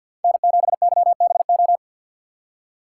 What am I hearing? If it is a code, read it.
Morse code, heard as a keyed tone, about 51 wpm, 700 Hz. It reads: N7YBO
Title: USA Call Signs - 5 Characters - 4x Character Spacing - 50wpm